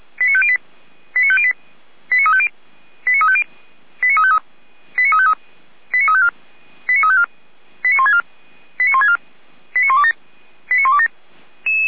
Five-tone selective call (ZVEI), heard as quick runs of five tones, tones of about 70 ms. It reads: 89508 89508 89350 89350 89353 89353 89354 89354 89164 89164 89127 89127